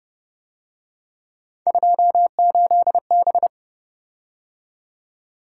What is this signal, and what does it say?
Morse code, heard as a keyed tone, about 28 wpm, 700 Hz. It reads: 28B